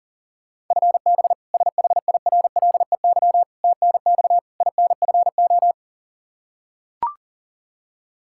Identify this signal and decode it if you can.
Morse code, heard as a keyed tone, about 40 wpm, 700 Hz. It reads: FB SHIRLEY TNX INFO
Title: K1USN SST QSO Practice 40wpm